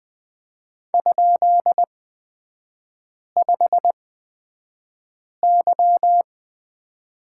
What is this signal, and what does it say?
Morse code, heard as a keyed tone, about 20 wpm, spaced as for 5 wpm, 700 Hz. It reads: ?5Y
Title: 236 Lesson 38e - Letter q - Three Characters 20wpm